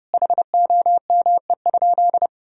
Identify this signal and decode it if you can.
Morse code, heard as a keyed tone, about 30 wpm, 700 Hz. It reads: HOME?
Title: Sentences from Top 1000 Words - Review Part 4 - 30wpm